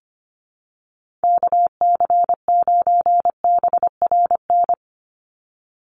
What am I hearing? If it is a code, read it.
Morse code, heard as a keyed tone, about 25 wpm, 700 Hz. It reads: KC9BRN
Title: USA Call Signs 25wpm